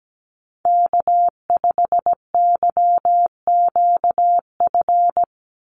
Morse code, heard as a keyed tone, about 17 wpm, 700 Hz. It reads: K5YQF
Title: Call Signs - 17wpm